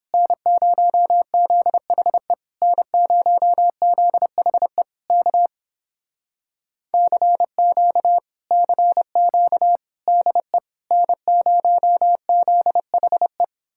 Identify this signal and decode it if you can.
Morse code, heard as a keyed tone, about 29 wpm, 700 Hz. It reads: N0ZHE N0ZHE K CQ CQ DE N0ZHE